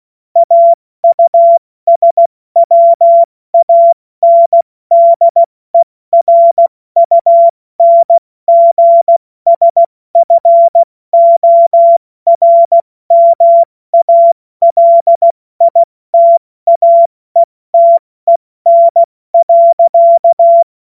Morse code, heard as a keyed tone, about 16 wpm, 680 Hz. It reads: AUSWANDERUNGSFORMALITAETEN.
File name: Morsekurs / Witze / Witz 19. April 2022 70 F80